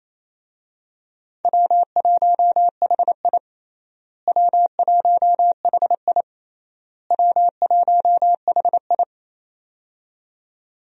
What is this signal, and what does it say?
Morse code, heard as a keyed tone, about 28 wpm, 700 Hz. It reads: W1HI W1HI W1HI
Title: USA Call Signs - 4 Characters - Repeated 3x - 28wpm